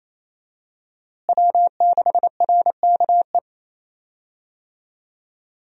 Morse code, heard as a keyed tone, about 28 wpm, 700 Hz. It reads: W6RKE
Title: USA Call Signs - 5 Characters - Repeated 3x - 28wpm